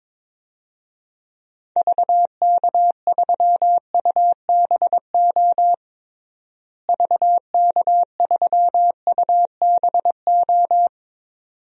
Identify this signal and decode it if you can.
Morse code, heard as a keyed tone, about 22 wpm, 700 Hz. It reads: VK3UBO VK3UBO
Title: DXCC-100 Call Signs - 6 Characters - Repeated 2x - 22wpm